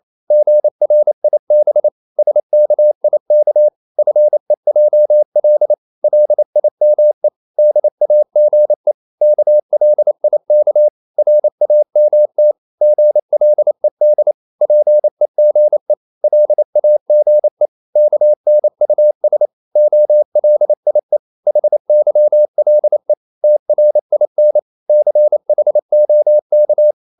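Morse code, heard as a keyed tone, about 28 wpm, 600 Hz. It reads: GRIB SKIK FEJL LIME DAGE KLIK RAMT GLED PEGE LAGE KNUS OLIE HYLE TRIN CHOK
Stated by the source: Danish short words sent in Morse at 28 wpm, from CW med Gnister.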